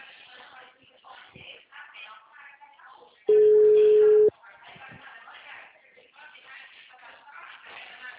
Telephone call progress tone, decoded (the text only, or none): ringback tone